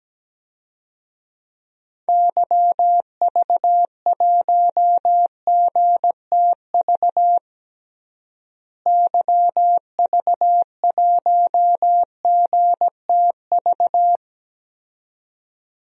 Morse code, heard as a keyed tone, about 17 wpm, 700 Hz. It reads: YV1GTV YV1GTV
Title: DXCC-100 Call Signs - 6 Characters - Repeated 2x - 17wpm